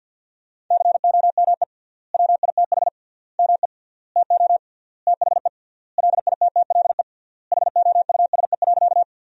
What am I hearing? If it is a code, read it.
Morse code, heard as a keyed tone, about 50 wpm, 700 Hz. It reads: COME WITH ME TO THE LITTLE HOUSE.
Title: 3x Word-Spacing - Sentences from Top 200 Words - 50wpm